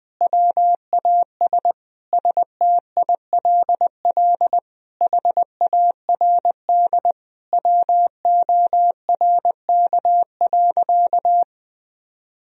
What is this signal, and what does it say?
Morse code, heard as a keyed tone, about 20 wpm, 700 Hz. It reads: WAS STILL HARD WORK.
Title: Sentences from Top 1000 Words - Review Part 3 - 20wpm